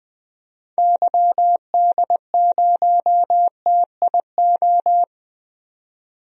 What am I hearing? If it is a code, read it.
Morse code, heard as a keyed tone, about 20 wpm, 700 Hz. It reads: YD0TIO
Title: DXCC-100 Call Signs - 6 Characters - 20wpm